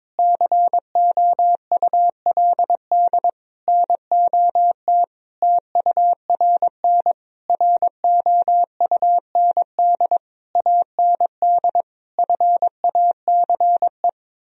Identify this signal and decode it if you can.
Morse code, heard as a keyed tone, about 22 wpm, 700 Hz. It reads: COULD NOT TURN ROUND AND FACE
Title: Sentences from Top 1000 Words - Review Part 2 - 22wpm